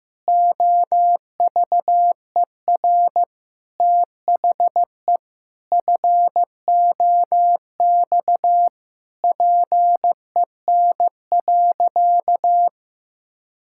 Morse code, heard as a keyed tone, about 15 wpm, 700 Hz. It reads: OVER THE FOX PEN.
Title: Pangram Challenge v2 - 15wpm